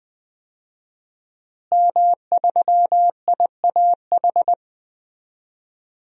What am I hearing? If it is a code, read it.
Morse code, heard as a keyed tone, about 20 wpm, 700 Hz. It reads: M3IAH
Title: DXCC-100 Call Signs - 5 Characters - Repeated 2x - 20wpm